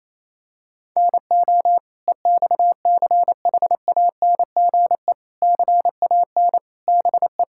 Morse code, heard as a keyed tone, about 28 wpm, 700 Hz. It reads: NO EXCHANGE CAN BE
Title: Sentences from Top 1000 Words - Review Part 6 - 28wpm